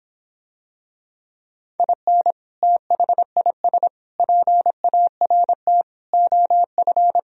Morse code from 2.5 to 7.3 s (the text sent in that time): THIS PART OF